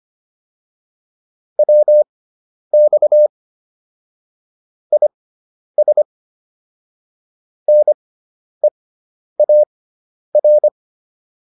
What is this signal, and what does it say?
Morse code, heard as a keyed tone, about 25 wpm, spaced as for 10 wpm, 600 Hz. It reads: WX IS NEAR